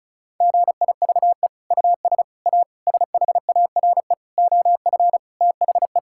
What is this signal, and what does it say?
Morse code, heard as a keyed tone, about 34 wpm, 700 Hz. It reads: GIVE US A SHARE OF THE
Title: Sentences from Top 1000 Words - Review Part 6 - 35wpm